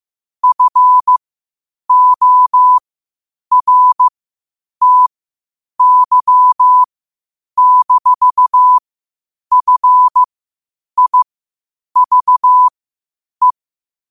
Morse code, heard as a keyed tone, about 15 wpm, spaced as for 8 wpm, 1000 Hz. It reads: FORTY-FIVE
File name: qotdfiles / 8wpm